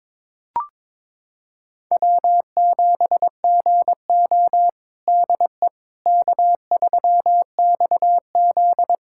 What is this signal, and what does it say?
Morse code, heard as a keyed tone, about 21 wpm, 700 Hz. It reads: W7GO DE K3XZ
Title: USA Callsign DE Callsign - 4 Characters - 22wpm